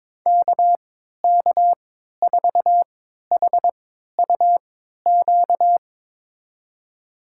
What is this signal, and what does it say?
Morse code, heard as a keyed tone, about 22 wpm, spaced as for 12 wpm, 700 Hz. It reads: KK4HUQ
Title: USA Call Signs - 6 Characters - 3x Character Spacing - 22wpm